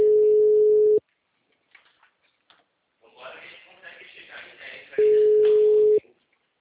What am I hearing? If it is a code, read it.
Ringback tone (Europe, 425 Hz), with speech.